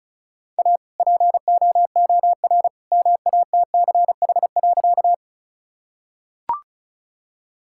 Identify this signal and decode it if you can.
Morse code, heard as a keyed tone, about 34 wpm, 700 Hz. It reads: A POOR MATCH.
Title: Sentences from Top 1600 Words 35wpm